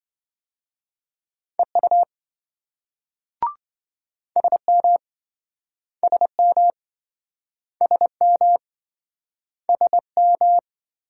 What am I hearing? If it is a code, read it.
Morse code, heard as a keyed tone, about 27 wpm, 700 Hz. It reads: EU SM SM SM SM